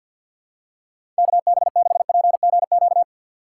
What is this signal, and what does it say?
Morse code, heard as a keyed tone, about 48 wpm, 700 Hz. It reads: KB6PGQ
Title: USA Call Signs - 6 Characters - 4x Character Spacing - 50wpm